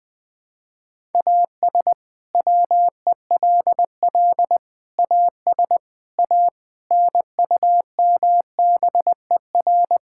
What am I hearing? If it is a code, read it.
Morse code, heard as a keyed tone, about 20 wpm, 700 Hz. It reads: AS WELL AS A NUMBER
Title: Sets of 6 Words - 20wpm